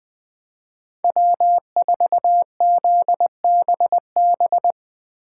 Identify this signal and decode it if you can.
Morse code, heard as a keyed tone, about 20 wpm, 700 Hz. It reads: W4ZBB
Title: Speed Racing - USA 1x3 Call Signs 17wpm